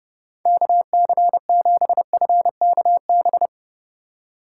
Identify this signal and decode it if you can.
Morse code, heard as a keyed tone, about 29 wpm, 700 Hz. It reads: KC7FKB